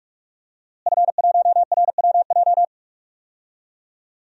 Morse code, heard as a keyed tone, about 46 wpm, 700 Hz. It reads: F1RWJ
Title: DXCC-100 Call Signs - 5 Characters - Repeated 3x - 45wpm